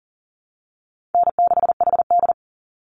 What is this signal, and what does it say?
Morse code, heard as a keyed tone, about 40 wpm, 700 Hz. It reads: N6HD
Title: Easy Call Signs Repeated 2x - 40wpm